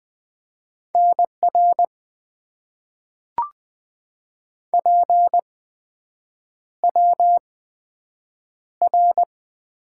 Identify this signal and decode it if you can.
Morse code, heard as a keyed tone, about 20 wpm, spaced as for 7 wpm, 700 Hz. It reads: NR PWR
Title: Top 90 QSO Elements - 8x Character Spacing - 20wpm